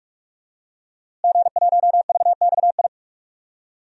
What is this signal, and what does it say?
Morse code, heard as a keyed tone, about 44 wpm, 700 Hz. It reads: G1VXI
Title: DXCC-100 Call Signs - 5 Characters - Repeated 3x - 45wpm